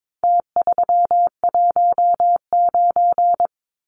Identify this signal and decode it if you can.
Morse code, heard as a keyed tone, about 22 wpm, 700 Hz. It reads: T319
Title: Numbers - Contest Style - 4-Digit 22wpm